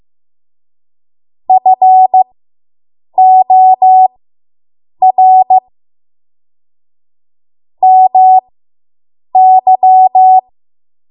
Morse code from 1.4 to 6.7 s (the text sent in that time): FOR